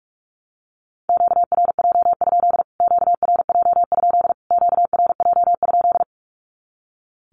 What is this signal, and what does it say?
Morse code, heard as a keyed tone, about 44 wpm, 700 Hz. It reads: QRJ? QRJ? QRJ?
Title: Q-Codes Repeated 3x 45wpm